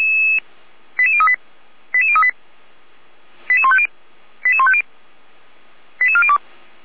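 Five-tone selective call (ZVEI), heard as quick runs of five tones, tones of about 70 ms. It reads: 88038 88038 88150 88150 88402
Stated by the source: ZVEI five-tone selective call 89235 received by SDR, from the Feuerwehr Orlamünde audio server.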